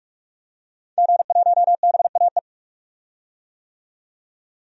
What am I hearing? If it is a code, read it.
Morse code, heard as a keyed tone, about 45 wpm, 700 Hz. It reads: G1BAE
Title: DXCC-100 Call Signs - 5 Characters - 45wpm